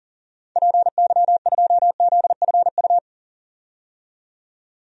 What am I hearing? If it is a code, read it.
Morse code, heard as a keyed tone, about 37 wpm, 700 Hz. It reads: PY2ZFU
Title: DXCC-100 Call Signs - 6 Characters - 40wpm